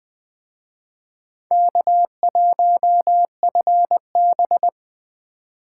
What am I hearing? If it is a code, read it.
Morse code, heard as a keyed tone, about 25 wpm, 700 Hz. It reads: K1FB
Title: Speed Racing - USA Call Signs 20wpm